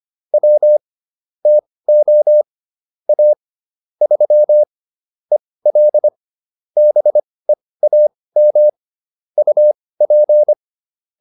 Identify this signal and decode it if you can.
Morse code, heard as a keyed tone, about 25 wpm, spaced as for 18 wpm, 600 Hz. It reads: W TO A 3 EL BEAM UP